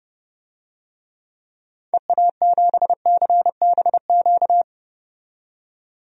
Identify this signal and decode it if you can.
Morse code, heard as a keyed tone, about 30 wpm, 700 Hz. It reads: EA7CBQ
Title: DXCC-100 Call Signs - 6 Characters - 30wpm